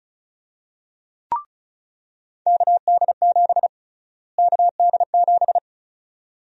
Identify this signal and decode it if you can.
Morse code, heard as a keyed tone, about 36 wpm, 700 Hz. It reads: KD7 KD7
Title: USA Call Sign Prefixes - Repeated 2x - 35wpm